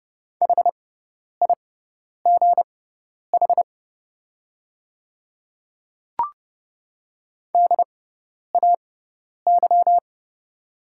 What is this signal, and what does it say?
Morse code, heard as a keyed tone, about 30 wpm, spaced as for 10 wpm, 690 Hz. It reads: HIGH DAY
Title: 6x Character Spacing - Top 200 Words - 30wpm